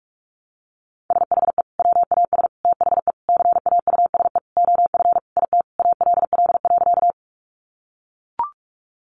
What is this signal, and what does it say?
Morse code, heard as a keyed tone, about 44 wpm, 700 Hz. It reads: SHE WAS THE CAUSE OF IT ALL.
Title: Sentences from Top 200 Words 45wpm